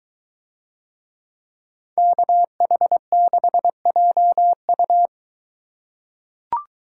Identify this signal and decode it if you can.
Morse code, heard as a keyed tone, about 23 wpm, 700 Hz. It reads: KH6JU